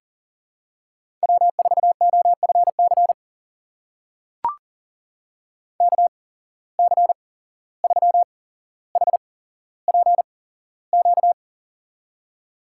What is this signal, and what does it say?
Morse code, heard as a keyed tone, about 38 wpm, spaced as for 16 wpm, 700 Hz. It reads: W4OFC KC3HPQ
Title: USA Call Signs - 8x Character Spacing - 40wpm